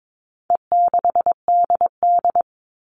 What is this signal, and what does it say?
Morse code, heard as a keyed tone, about 22 wpm, 700 Hz. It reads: E6DD
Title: All Cut Numbers - 4-Digit 22wpm